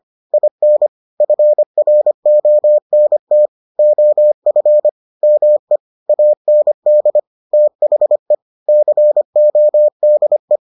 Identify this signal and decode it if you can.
Morse code, heard as a keyed tone, about 25 wpm, 600 Hz. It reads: IN FRONT OF ME AND THE CODE